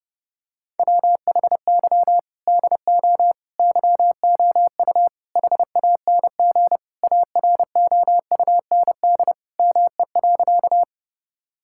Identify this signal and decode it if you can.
Morse code, heard as a keyed tone, about 30 wpm, 700 Hz. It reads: WHY DO YOU HANG AROUND ME.
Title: Sentences from Top 1600 Words 30wpm